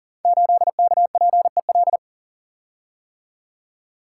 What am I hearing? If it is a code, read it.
Morse code, heard as a keyed tone, about 42 wpm, 700 Hz. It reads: 8KPEL
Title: Code Groups - 5 Letter:Numbers - 40wpm